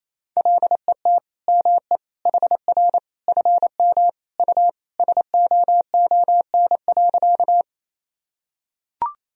Morse code, heard as a keyed tone, about 27 wpm, 700 Hz. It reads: LET ME HR FM U SOON.
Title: Sentences from Top 400 Words and Abbreviations 28wpm